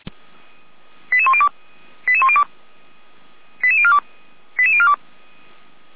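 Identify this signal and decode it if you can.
Five-tone selective call (ZVEI), heard as quick runs of five tones, tones of about 70 ms. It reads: 88102 88102 88042 88042
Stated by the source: ZVEI five-tone call received on SDR, from feuerwehr-orlamuende fire service server.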